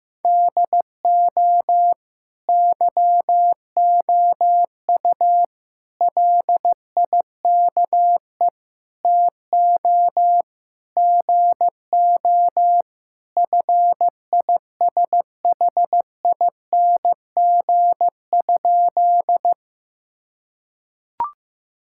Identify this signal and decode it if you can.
Morse code, heard as a keyed tone, about 16 wpm, 700 Hz. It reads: DO YOU LIKE TO GO FISHING?
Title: Sentences from Top 1000 Words - Review Part 5 - 15wpm